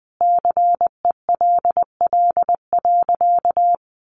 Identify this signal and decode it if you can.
Morse code, heard as a keyed tone, about 20 wpm, 700 Hz. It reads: CELL.
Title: Sentences from Top 1100 Words 20wpm